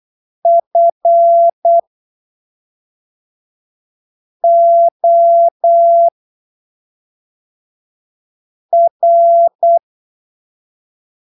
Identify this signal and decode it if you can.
Morse code, heard as a keyed tone, about 8 wpm, spaced as for 3 wpm, 680 Hz. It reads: FOR
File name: arquivos / texto / texto 18 08ppm